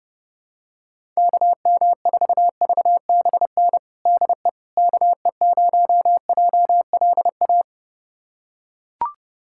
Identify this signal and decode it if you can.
Morse code, heard as a keyed tone, about 30 wpm, 700 Hz. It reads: KM4VBN DE KE0JLA
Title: USA Callsign DE Callsign - 6 Characters - 30wpm